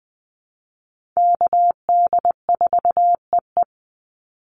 Morse code, heard as a keyed tone, about 20 wpm, 700 Hz. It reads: KD4EE
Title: Call Signs - 20wpm